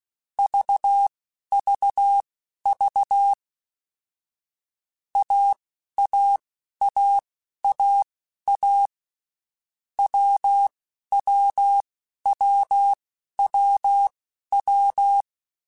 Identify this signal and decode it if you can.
Morse code, heard as a keyed tone, about 16 wpm, spaced as for 11 wpm, 800 Hz. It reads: VVV AAAAA WWWWW
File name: Basic Letter Groups By7Groups / AWUNDG / AWUNDG@10